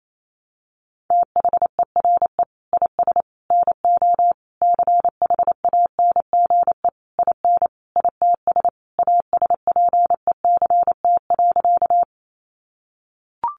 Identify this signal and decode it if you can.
Morse code, heard as a keyed tone, about 28 wpm, 700 Hz. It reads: THERE IS NO CHANGE IN ITS ASPECT.
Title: Sentences from Top 1300 Words 28wpm